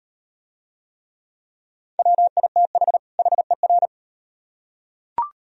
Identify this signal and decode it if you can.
Morse code, heard as a keyed tone, about 38 wpm, 700 Hz. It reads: WITH HER